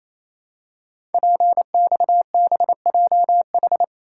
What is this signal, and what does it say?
Morse code, heard as a keyed tone, about 27 wpm, 700 Hz. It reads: PXBJH
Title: Code Groups - 5 Letters - 28wpm